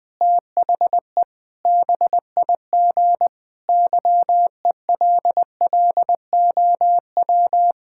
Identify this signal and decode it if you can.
Morse code, heard as a keyed tone, about 20 wpm, 700 Hz. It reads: THE BIG YELLOW
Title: Pangram Challenge v2 - 20wpm